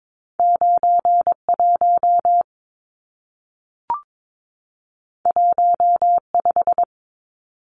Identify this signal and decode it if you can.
Morse code, heard as a keyed tone, about 22 wpm, 700 Hz. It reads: 91 15
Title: Numbers - 2-Digit 22wpm